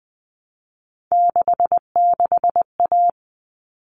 Morse code, heard as a keyed tone, about 20 wpm, 700 Hz. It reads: 66A
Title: All Cut Numbers - 3-Digit 20wpm